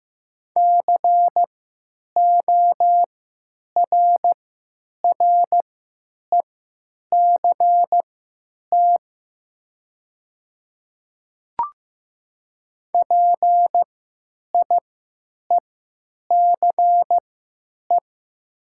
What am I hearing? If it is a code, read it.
Morse code, heard as a keyed tone, about 15 wpm, spaced as for 9 wpm, 700 Hz. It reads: CORRECT PIECE
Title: 3x Character Spacing - Top 500 Words - 15wpm